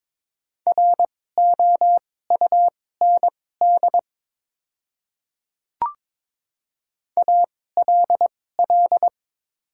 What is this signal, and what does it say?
Morse code, heard as a keyed tone, about 22 wpm, spaced as for 16 wpm, 700 Hz. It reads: ROUND ALL
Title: 2x Character Spacing - Top 500 Words - 22wpm